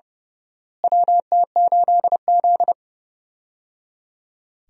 Morse code, heard as a keyed tone, about 30 wpm, 700 Hz. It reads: WT8Z
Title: USA Call Signs - 4 Characters - Repeated 3x - 30wpm